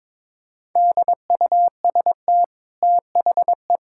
Morse code, heard as a keyed tone, about 22 wpm, 690 Hz. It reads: DUST THE
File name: practice / 22wpm / Easy Sentences with Hints - 22wpm